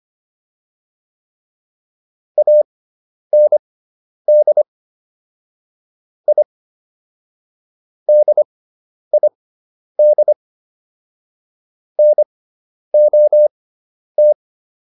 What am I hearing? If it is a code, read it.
Morse code, heard as a keyed tone, about 25 wpm, spaced as for 10 wpm, 600 Hz. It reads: AND I DID NOT